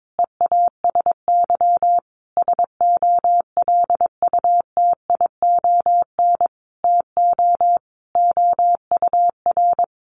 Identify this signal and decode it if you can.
Morse code, heard as a keyed tone, about 22 wpm, 700 Hz. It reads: EASY SOLUTION TO OUR
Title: Sentences from Top 1300 Words 22wpm